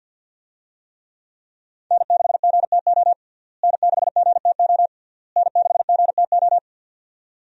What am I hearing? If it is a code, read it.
Morse code, heard as a keyed tone, about 50 wpm, 700 Hz. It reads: N6GTO N6GTO N6GTO
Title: USA Call Signs - 5 Characters - Repeated 3x - 50wpm